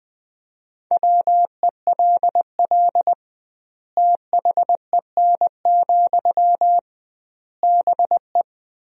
Morse code, heard as a keyed tone, about 19 wpm, 700 Hz. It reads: WELL THEN, BE
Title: 2x Word-Spacing - Sentences from Top 500 Words - 20wpm